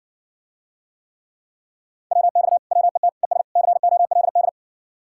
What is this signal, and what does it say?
Morse code, heard as a keyed tone, about 58 wpm, 700 Hz. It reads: WX WET ES COLD